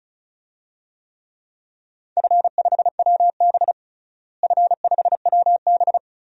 Morse code, heard as a keyed tone, about 36 wpm, 700 Hz. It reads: F5WB F5WB
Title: DXCC-100 Call Signs - 4 Characters - Repeated 2x - 35wpm